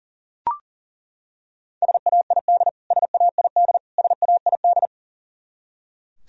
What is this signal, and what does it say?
Morse code, heard as a keyed tone, about 38 wpm, 700 Hz. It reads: SAID SAID SAID
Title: Top 100 Words Repeated 40wpm